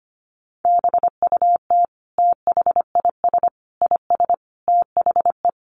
Morse code, heard as a keyed tone, about 25 wpm, 700 Hz. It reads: BUT THIS IS THE
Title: Sentences from Top 200 Words 25wpm